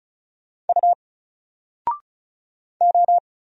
Morse code, heard as a keyed tone, about 34 wpm, 700 Hz. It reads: U O
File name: Single-Letter-Number-35wpm